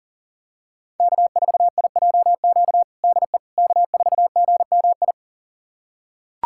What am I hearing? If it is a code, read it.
Morse code, heard as a keyed tone, about 40 wpm, 700 Hz. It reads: K4IJQ DE K4GMI
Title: USA Callsign DE Callsign 40wpm